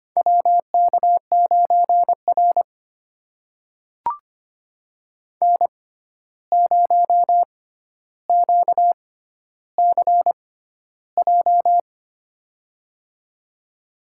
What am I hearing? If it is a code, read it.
Morse code, heard as a keyed tone, about 25 wpm, spaced as for 12 wpm, 700 Hz. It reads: WK9R N0QCJ
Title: USA Call Signs - 6x Character Spacing - 25wpm